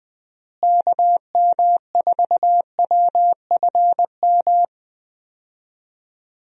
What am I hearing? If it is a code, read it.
Morse code, heard as a keyed tone, about 20 wpm, 700 Hz. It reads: KM4WFM
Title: USA Call Signs - 6 Characters - Repeated 3x - 20wpm